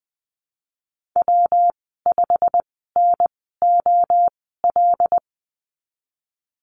Morse code, heard as a keyed tone, about 20 wpm, spaced as for 14 wpm, 700 Hz. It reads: W5NOL
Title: 173 Lesson 28g - Number 9 - Call Signs 20wpm